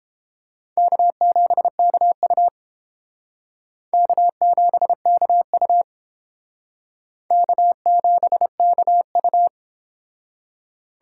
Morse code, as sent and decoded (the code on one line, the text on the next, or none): -.- --... -.- ..- / -.- --... -.- ..- / -.- --... -.- ..-
K7KU K7KU K7KU